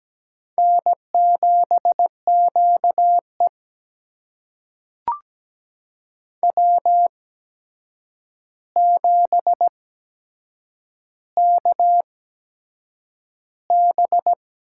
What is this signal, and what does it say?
Morse code, heard as a keyed tone, about 17 wpm, spaced as for 7 wpm, 700 Hz. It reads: N7QE W7KB
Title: USA Call Signs - 4 Characters - 8x Character Spacing - 17wpm